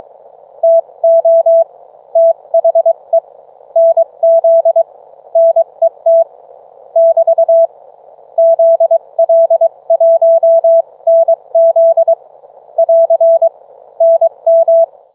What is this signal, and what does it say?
Morse code, heard as a keyed tone, about 23 wpm, spaced as for 20 wpm, 660 Hz. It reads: TO THE NZ NET <BT> ZL1NZ <AR> NM